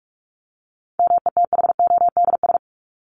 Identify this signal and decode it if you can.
Morse code, heard as a keyed tone, about 44 wpm, 700 Hz. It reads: METHODS
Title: ES:S Suffix - Top 100 Words with Base Words 45wpm